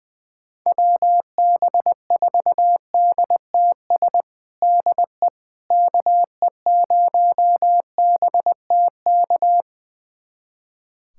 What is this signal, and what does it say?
Morse code, heard as a keyed tone, about 21 wpm, 700 Hz. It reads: WB4DTS DE KE0BTK
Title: USA Callsign DE Callsign - 6 Characters - 20wpm